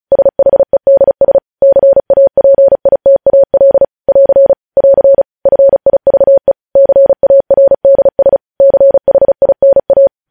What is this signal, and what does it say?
Morse code, heard as a keyed tone, about 36 wpm, 550 Hz. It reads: SHEDS CAPITAL <AR> <AR> FIVE CARDS CHINA
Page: news headlines in morse code at 35 wpm